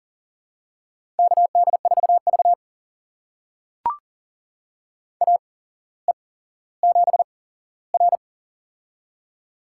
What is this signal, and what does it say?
Morse code, heard as a keyed tone, about 42 wpm, spaced as for 17 wpm, 700 Hz. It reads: KD4V AE7R
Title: USA Call Signs - 4 Characters - 8x Character Spacing - 40wpm